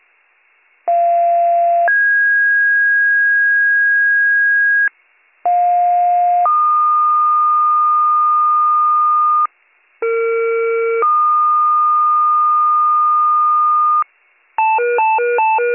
Two-tone page, heard as a one-second tone, then a three-second tone, about 690 then 1650 Hz, about 710 then 1220 Hz, about 460 then 1220 Hz, with noise. It is heard from 0.9 to 14.0 s.